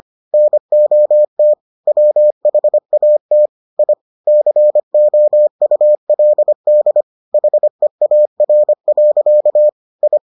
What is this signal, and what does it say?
Morse code, heard as a keyed tone, about 25 wpm, 600 Hz. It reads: NOT WHAT I COULD HEAR. I